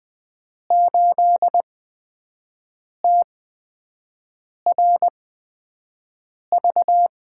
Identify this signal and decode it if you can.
Morse code, heard as a keyed tone, about 20 wpm, spaced as for 5 wpm, 700 Hz. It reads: 8TRV
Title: USA Call Signs - 5 Characters - 8x Character Spacing - 20wpm